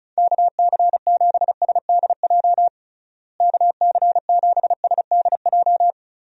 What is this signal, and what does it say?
Morse code, heard as a keyed tone, about 34 wpm, 700 Hz. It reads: KC7SDJ KC7SDJ
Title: USA Call Signs - 6 Characters - Repeated 3x - 35wpm